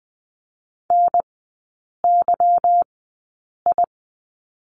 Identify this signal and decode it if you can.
Morse code, heard as a keyed tone, about 20 wpm, spaced as for 8 wpm, 690 Hz. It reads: NYI
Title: 145 Lesson 24e - Letter y - Three Characters 20wpm